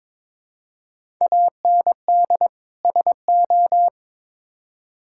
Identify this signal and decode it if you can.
Morse code, heard as a keyed tone, about 22 wpm, 700 Hz. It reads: AND SO
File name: Build A Phrase - 2-Word Phrases - 22wpm